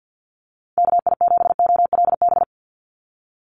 Morse code, heard as a keyed tone, about 50 wpm, 700 Hz. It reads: KI7OLB
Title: USA Call Signs 50wpm